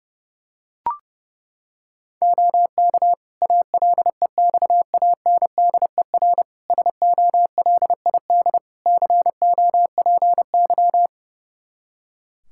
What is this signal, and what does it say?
Morse code, heard as a keyed tone, about 30 wpm, 700 Hz. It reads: OK ALEXANDER SOLID COPY
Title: K1USN SST QSO Practice 30wpm